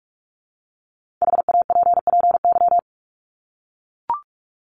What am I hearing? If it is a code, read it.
Morse code, heard as a keyed tone, about 44 wpm, 700 Hz. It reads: HAPPY
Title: Top 751-1000 Words 45wpm